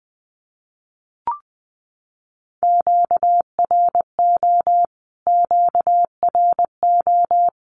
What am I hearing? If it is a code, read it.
Morse code, heard as a keyed tone, about 20 wpm, 700 Hz. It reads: QRO QRO
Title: Q-Codes Repeated 3x 20wpm